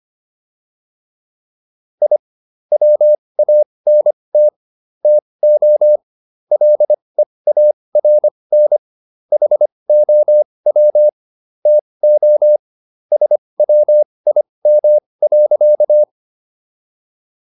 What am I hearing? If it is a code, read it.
Morse code, heard as a keyed tone, about 25 wpm, spaced as for 20 wpm, 600 Hz. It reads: I WANT TO LEARN HOW TO SWIM.